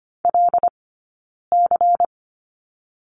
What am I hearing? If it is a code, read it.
Morse code, heard as a keyed tone, about 25 wpm, 700 Hz. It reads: L C
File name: course / 25:10wpm / 092 Lesson 16d - Letter c - Two Characters 25wpm